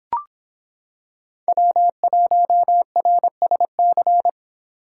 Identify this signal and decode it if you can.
Morse code, heard as a keyed tone, about 26 wpm, 700 Hz. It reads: W1RSC